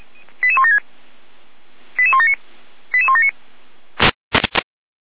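Five-tone selective call (ZVEI), heard as quick runs of five tones, tones of about 70 ms. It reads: 88176 88179 88179
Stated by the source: Five-tone ZVEI selective call received by SDR, channel 455.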